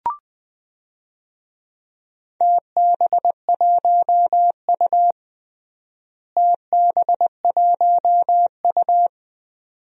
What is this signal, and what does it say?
Morse code, heard as a keyed tone, about 20 wpm, 700 Hz. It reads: TB1U TB1U
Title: DXCC-100 Call Signs - 4 Characters - Repeated 2x - 20wpm